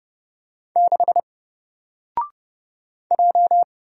Morse code, heard as a keyed tone, about 30 wpm, 700 Hz. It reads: 6 J
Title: Single Character (Letter:Number:Punctuation:Prosign) 30wpm